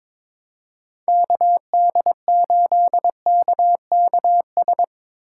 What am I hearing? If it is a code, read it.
Morse code, heard as a keyed tone, about 22 wpm, 700 Hz. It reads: KD8KKS
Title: USA Call Signs - 6 Characters - 2x Character Spacing - 22wpm